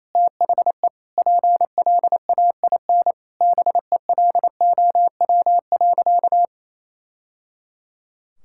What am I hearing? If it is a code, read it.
Morse code, heard as a keyed tone, about 28 wpm, 700 Hz. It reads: THE PLAIN BELOW.
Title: Sentences from Top 1000 Words - Review Part 5 - 28wpm